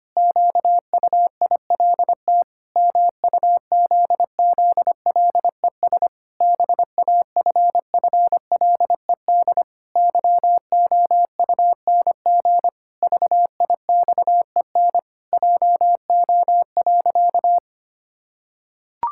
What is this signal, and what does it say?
Morse code, heard as a keyed tone, about 25 wpm, 700 Hz. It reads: QUILT MUZZLES BAFFLED YOUNG VIXEN JO.
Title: Pangram Challenge v2 - 25wpm